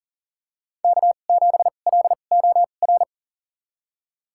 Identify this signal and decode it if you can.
Morse code, heard as a keyed tone, about 40 wpm, spaced as for 29 wpm, 700 Hz. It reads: K7LOR
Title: USA Call Signs - 5 Characters - 2x Character Spacing - 40wpm